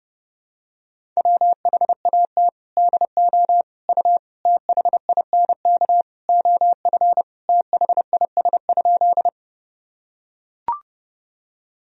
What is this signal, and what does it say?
Morse code, heard as a keyed tone, about 32 wpm, 700 Hz. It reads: WHAT DO U THINK OF THIS?
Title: Sentences from Top 200 Words and Abbreviations 30wpm